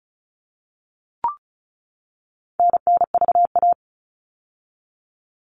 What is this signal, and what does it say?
Morse code, heard as a keyed tone, about 34 wpm, 700 Hz. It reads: NNVA
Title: All Cut Numbers - 4-Digit 35wpm